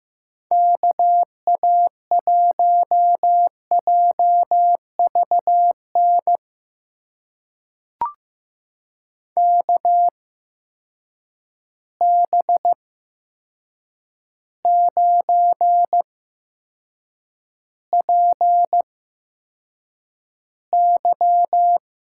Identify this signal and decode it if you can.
Morse code, heard as a keyed tone, about 15 wpm, spaced as for 7 wpm, 700 Hz. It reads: KA1JVN KB9PY
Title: USA Call Signs - 6 Characters - 8x Character Spacing - 15wpm